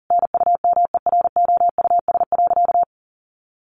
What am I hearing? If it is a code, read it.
Morse code, heard as a keyed tone, about 37 wpm, 700 Hz. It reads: NUMEROUS.
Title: Sentences from Top 1400 Words 40wpm